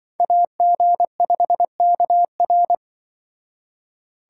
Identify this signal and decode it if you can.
Morse code, heard as a keyed tone, about 24 wpm, 700 Hz. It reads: AG5KR
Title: Speed Racing - USA Call Signs 20wpm